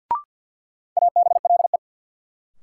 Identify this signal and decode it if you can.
Morse code, heard as a keyed tone, about 52 wpm, 700 Hz. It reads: ABLE
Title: Sentences from Top 700 Words - New Word Drill - 50wpm